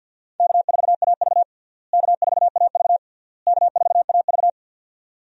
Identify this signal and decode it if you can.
Morse code, heard as a keyed tone, about 47 wpm, 700 Hz. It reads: K4AV K4AV K4AV
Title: USA Call Signs - 4 Characters - Repeated 3x - 50wpm